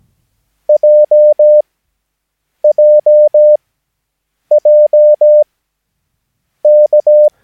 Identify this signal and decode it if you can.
Morse code, heard as a keyed tone, about 17 wpm, spaced as for 7 wpm, 600 Hz. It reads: JJJK
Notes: Övningsmaterial med alfabetet A till Ö samt siffror 0 till 9 enligt International/Continental Morse Code.